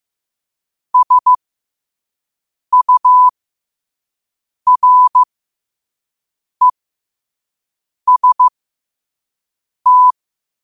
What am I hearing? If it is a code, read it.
Morse code, heard as a keyed tone, about 15 wpm, spaced as for 5 wpm, 1000 Hz. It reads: SUREST
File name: qotdfiles / 5wpm